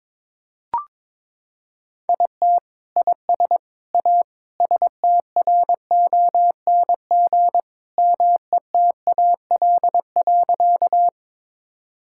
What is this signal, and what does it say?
Morse code, heard as a keyed tone, about 21 wpm, 700 Hz. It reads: IT IS A STRONG METAL.
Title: Sentences from Top 1000 Words - Review Part 6 - 22wpm